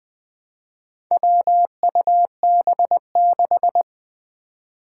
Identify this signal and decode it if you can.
Morse code, heard as a keyed tone, about 21 wpm, 700 Hz. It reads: WUB6
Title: Four Challenging Letter-Number Combinations 20wpm